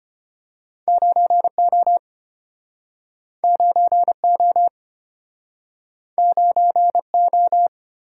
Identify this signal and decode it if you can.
Morse code, heard as a keyed tone, about 32 wpm, 700 Hz. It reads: 9O 9O 9O